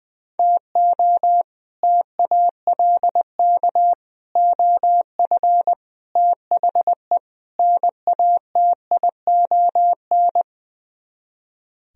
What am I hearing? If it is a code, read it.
Morse code, heard as a keyed tone, about 20 wpm, 700 Hz. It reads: TO TALK OF THE NATION
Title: Speed-Racing - Sets of 5 Words 20wpm